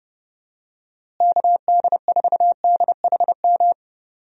Morse code, heard as a keyed tone, about 30 wpm, 700 Hz. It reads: KD4DHM
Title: USA Call Signs - 6 Characters - Repeated 2x - 30wpm